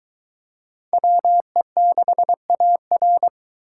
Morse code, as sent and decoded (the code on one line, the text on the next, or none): .-- . -.... .- .-.
WE6AR